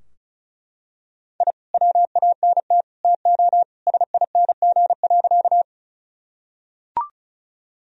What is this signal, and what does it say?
Morse code, heard as a keyed tone, about 36 wpm, 700 Hz. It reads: I WANT TO SING.
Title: Sentences from Top 1000 Words - Review Part 2 - 35wpm